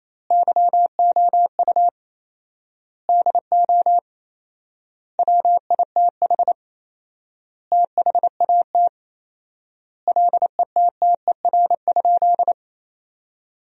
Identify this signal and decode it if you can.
Morse code, heard as a keyed tone, about 28 wpm, 700 Hz. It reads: YOU DO WITH THAT LETTER?
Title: 4x Word-Spacing - Sentences from Top 300 Words - 28wpm